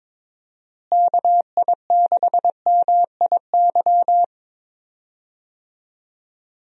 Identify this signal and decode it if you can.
Morse code, heard as a keyed tone, about 22 wpm, 700 Hz. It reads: KI6MIY
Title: USA Call Signs - 6 Characters - Repeated 2x - 22wpm